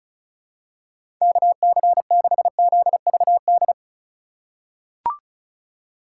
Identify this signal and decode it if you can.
Morse code, heard as a keyed tone, about 34 wpm, 700 Hz. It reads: KC6ZVD
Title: USA Call Signs - 6 Characters - 3x Character Spacing - 35wpm